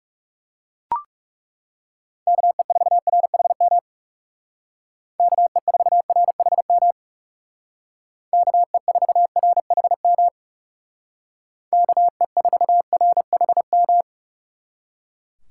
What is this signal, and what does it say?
Morse code, heard as a keyed tone, about 39 wpm, 700 Hz. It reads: KE4RHM KE4RHM KE4RHM KE4RHM